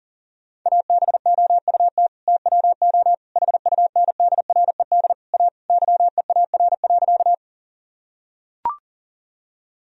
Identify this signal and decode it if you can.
Morse code, heard as a keyed tone, about 40 wpm, 700 Hz. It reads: ABOUT TWO HUNDRED A YEAR.
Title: Sentences from Top 1000 Words - Review Part 2 - 40wpm